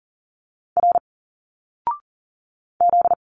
Morse code, heard as a keyed tone, about 39 wpm, 700 Hz. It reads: R Z
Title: Single Letters 40wpm